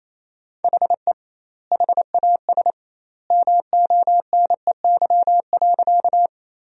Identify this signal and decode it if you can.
Morse code, heard as a keyed tone, about 28 wpm, 700 Hz. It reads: HE HAS MONEY.
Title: 2x Word-Spacing - Sentences from Top 400 Words - 28wpm